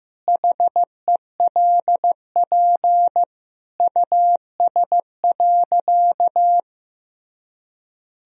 Morse code, heard as a keyed tone, about 15 wpm, 700 Hz. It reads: HELP US.
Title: Sentences from Top 1000 Words - Review Part 3 - 15wpm